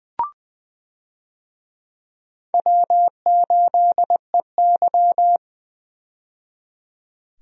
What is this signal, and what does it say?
Morse code, heard as a keyed tone, about 20 wpm, 700 Hz. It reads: W8EY
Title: USA Call Signs - 4 Characters - 20wpm